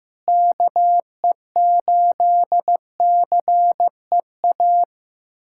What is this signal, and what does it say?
Morse code, heard as a keyed tone, about 15 wpm, 700 Hz. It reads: KE8CEA
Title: USA Callsign DE Callsign - 6 Characters - 15wpm